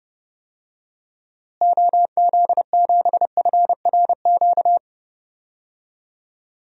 Morse code, heard as a keyed tone, about 30 wpm, 700 Hz. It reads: OZ7FRQ
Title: DXCC-100 Call Signs - 6 Characters - Repeated 3x - 30wpm